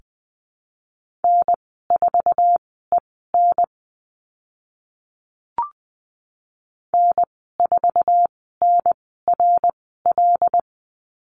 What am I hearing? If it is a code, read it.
Morse code, heard as a keyed tone, about 20 wpm, spaced as for 15 wpm, 700 Hz. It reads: N4EN N4NRL
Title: 076 Lesson 13g - Letter L - Call Signs 20wpm